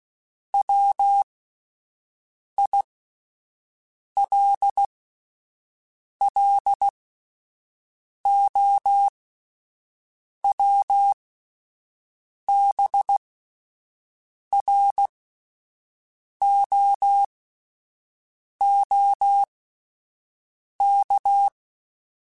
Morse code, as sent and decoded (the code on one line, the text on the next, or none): .-- .. .-.. .-.. --- .-- -... .-. --- --- -.-
WILLOWBROOK